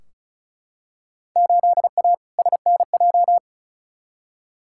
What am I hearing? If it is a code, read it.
Morse code, heard as a keyed tone, about 36 wpm, 700 Hz. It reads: 8A SNJ
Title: Field Day Exchanges Encore - 35wpm